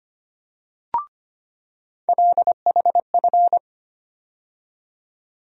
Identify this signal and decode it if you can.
Morse code, heard as a keyed tone, about 25 wpm, 700 Hz. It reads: LHF
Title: USA Call Sign Postfixes - 3 Letters 25wpm